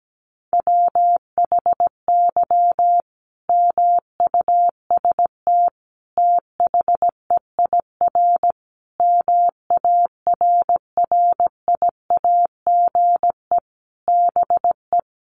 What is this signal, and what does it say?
Morse code, heard as a keyed tone, about 17 wpm, 700 Hz. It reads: WHY MUST THEIR MARRIAGE BE
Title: Sentences from Top 1000 Words 17wpm